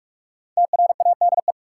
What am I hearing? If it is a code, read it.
Morse code, heard as a keyed tone, about 46 wpm, 700 Hz. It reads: TRADE
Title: Sets of 5 Words Encore - New Word Drill - 45wpm